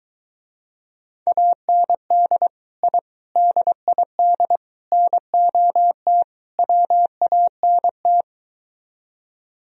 Morse code, heard as a keyed tone, about 23 wpm, 700 Hz. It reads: AND I DID NOT WANT